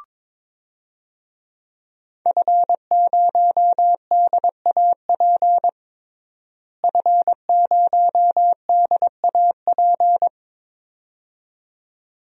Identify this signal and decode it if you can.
Morse code, heard as a keyed tone, about 21 wpm, 700 Hz. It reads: F0DAP F0DAP
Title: DXCC-100 Call Signs - 5 Characters - Repeated 2x - 22wpm